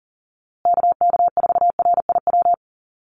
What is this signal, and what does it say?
Morse code, heard as a keyed tone, about 40 wpm, 700 Hz. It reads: KK4RIW
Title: USA Call Signs Repeated 3x 40wpm